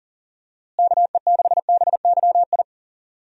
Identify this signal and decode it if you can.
Morse code, heard as a keyed tone, about 40 wpm, 700 Hz. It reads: KE6BYI
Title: USA Call Signs - 6 Characters - Repeated 3x - 40wpm